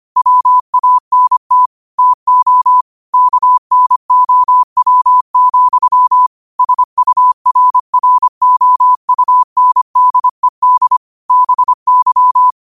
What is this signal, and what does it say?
Morse code, heard as a keyed tone, about 25 wpm, 1000 Hz. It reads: WANT TO KNOW, SURROUNDED BY
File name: qotdfiles / 25wpm